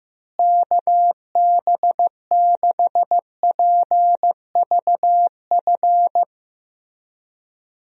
Morse code, heard as a keyed tone, about 15 wpm, 700 Hz. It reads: KB6PVF
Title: USA Call Signs - 6 Characters - Repeated 2x - 15wpm